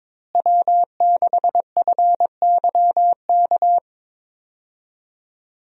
Morse code, heard as a keyed tone, about 21 wpm, 700 Hz. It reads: W6FYK
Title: USA Call Signs - 5 Characters - Repeated 2x - 22wpm